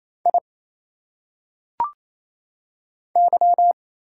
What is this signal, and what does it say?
Morse code, heard as a keyed tone, about 27 wpm, 700 Hz. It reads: I Y
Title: Single Letters - Phonetic Pronunciation 28wpm